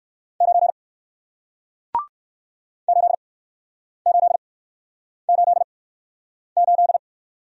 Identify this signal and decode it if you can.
Morse code, heard as a keyed tone, about 60 wpm, spaced as for 9 wpm, 700 Hz. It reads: 9 8888